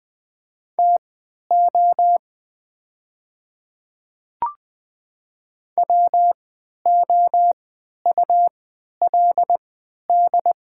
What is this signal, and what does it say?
Morse code, heard as a keyed tone, about 21 wpm, spaced as for 11 wpm, 700 Hz. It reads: TO WOULD
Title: 3x Character Spacing - Top 100 Words - 20wpm